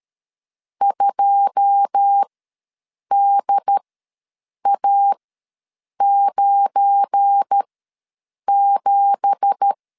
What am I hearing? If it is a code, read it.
Morse code, heard as a keyed tone, about 13 wpm, spaced as for 7 wpm, 800 Hz. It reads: 2DA97
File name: LEZIONI / 10 SIGNS / 10D COMA ws 12wpm 3618